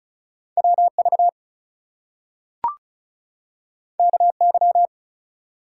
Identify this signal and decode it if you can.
Morse code, heard as a keyed tone, about 36 wpm, 700 Hz. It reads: WV KY
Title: US States and Canadian Provinces 35wpm